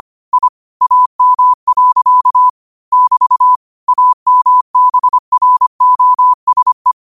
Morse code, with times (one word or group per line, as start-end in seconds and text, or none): 0.3-0.5 I
0.8-2.5 AM.
2.9-3.6 <BT>
3.9-6.9 AMBROSE